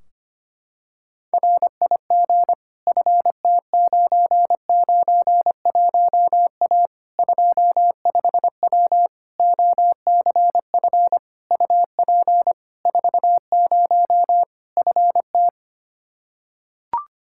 Morse code, heard as a keyed tone, about 25 wpm, 700 Hz. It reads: RIG FT991A 25W OCF UP 40 FT
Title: Practice QSO - Exchange 2 - Info - 25wpm